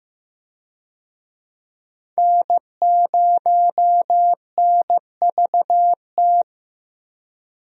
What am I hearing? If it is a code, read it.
Morse code, heard as a keyed tone, about 15 wpm, 700 Hz. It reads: N0NVT
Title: USA Call Signs - 5 Characters - 15wpm